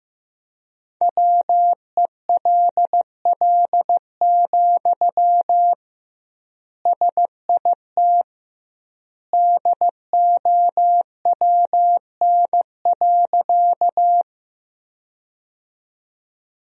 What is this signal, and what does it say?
Morse code, heard as a keyed tone, about 15 wpm, 700 Hz. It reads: WELL, SIT DOWN.
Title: 2x Word-Spacing - Sentences from Top 500 Words - 15wpm